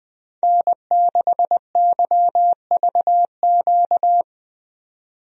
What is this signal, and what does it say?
Morse code, heard as a keyed tone, about 20 wpm, 700 Hz. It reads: N6YVQ
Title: USA Call Signs - 5 Characters - 20wpm